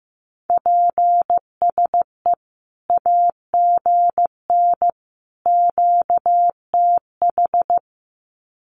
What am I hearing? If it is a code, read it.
Morse code, heard as a keyed tone, about 15 wpm, 700 Hz. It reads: PSE AGN QTH
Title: Common QSO Phrases 15wpm